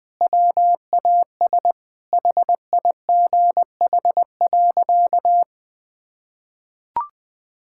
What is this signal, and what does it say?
Morse code, heard as a keyed tone, about 20 wpm, 700 Hz. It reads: WAS HIGH.
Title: Sentences with 3-Word Phrases - 20wpm